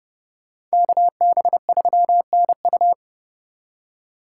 Morse code, heard as a keyed tone, about 38 wpm, 700 Hz. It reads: KB3NU